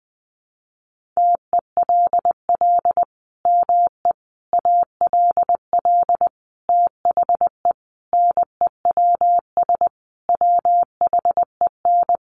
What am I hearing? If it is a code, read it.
Morse code, heard as a keyed tone, about 20 wpm, 700 Hz. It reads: TELL ME ALL THE NEWS WHEN
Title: Sentences from Top 800 Words 20wpm